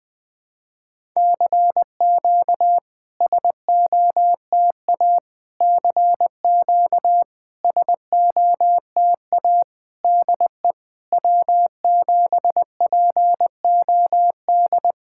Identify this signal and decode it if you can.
Morse code, heard as a keyed tone, about 20 wpm, 700 Hz. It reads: CQ SOTA CQ SOTA DE W7POD
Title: SOTA QSO Practice 20wpm